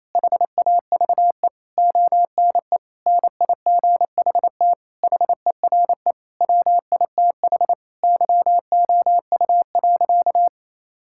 Morse code, heard as a keyed tone, about 28 wpm, 700 Hz. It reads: HAVE ONE NIGHT HERE WITH YOU.
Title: Sentences from Top 1600 Words 28wpm